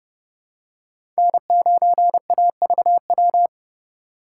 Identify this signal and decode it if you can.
Morse code, heard as a keyed tone, about 30 wpm, 700 Hz. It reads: N9AVW
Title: USA Call Signs - 5 Characters - 4x Character Spacing - 30wpm